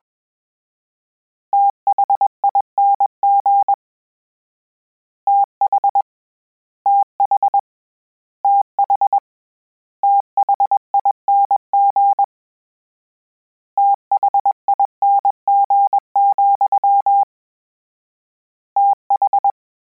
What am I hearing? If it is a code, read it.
Morse code, heard as a keyed tone, about 21 wpm, 800 Hz. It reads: THING TH TH TH THING THING, TH
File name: Combinations / Combinations-mp3s / Combinations@20